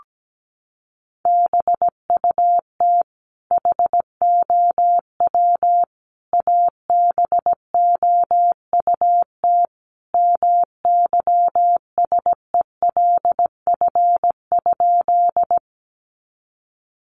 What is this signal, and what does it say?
Morse code, heard as a keyed tone, about 17 wpm, 700 Hz. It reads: BUT HOW ABOUT MYSELF?
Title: Sentences from Top 1300 Words 17wpm